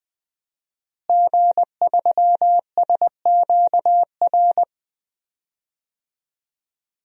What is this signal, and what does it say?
Morse code, heard as a keyed tone, about 20 wpm, 700 Hz. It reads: G3SQR
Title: DXCC-100 Call Signs - 5 Characters - Repeated 2x - 20wpm